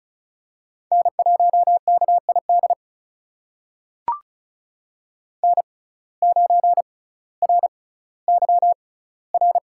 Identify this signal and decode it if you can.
Morse code, heard as a keyed tone, about 36 wpm, spaced as for 18 wpm, 700 Hz. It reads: N1KID N9RYR
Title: USA Call Signs - 6x Character Spacing - 35wpm